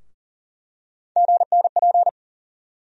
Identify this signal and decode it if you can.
Morse code, heard as a keyed tone, about 40 wpm, 700 Hz. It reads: GNP
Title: USA Call Sign Postfixes - 3 Letters 40wpm